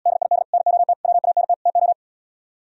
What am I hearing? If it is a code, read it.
Morse code, heard as a keyed tone, about 77 wpm, 700 Hz. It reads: DID NOT WANT TO